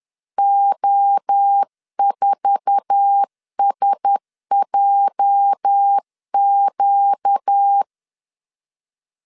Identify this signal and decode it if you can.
Morse code, heard as a keyed tone, about 10 wpm, 800 Hz. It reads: O4SJQ
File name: CW / DET / 1 9 1500C S B 10wpm 4439